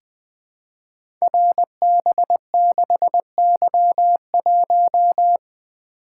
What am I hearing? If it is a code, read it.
Morse code, heard as a keyed tone, about 20 wpm, 700 Hz. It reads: RB6Y1
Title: Code Groups - 5 Letter:Numbers - 20wpm